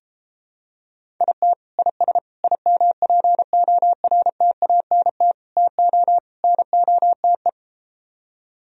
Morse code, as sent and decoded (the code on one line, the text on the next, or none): .. - / .. ... / .. -- .--. --- .-. - .- -. - / - --- / -. --- - .
IT IS IMPORTANT TO NOTE